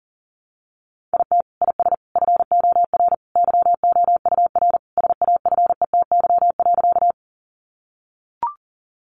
Morse code, heard as a keyed tone, about 40 wpm, 700 Hz. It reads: IT IS FOR YOUR SAFETY.
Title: Sentences from Top 700 Words 40wpm